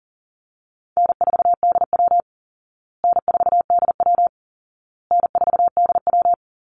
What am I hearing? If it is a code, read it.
Morse code, heard as a keyed tone, about 40 wpm, 700 Hz. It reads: N4DW N4DW N4DW
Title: Call Signs Repeated 2x - 40wpm